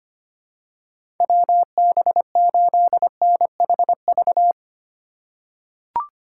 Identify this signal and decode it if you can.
Morse code, heard as a keyed tone, about 25 wpm, 700 Hz. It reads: WB8NHV
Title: USA Call Signs - 6 Characters - 3x Character Spacing - 25wpm